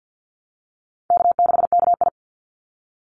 Morse code, heard as a keyed tone, about 50 wpm, 700 Hz. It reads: K6KI
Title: Call Signs - 50wpm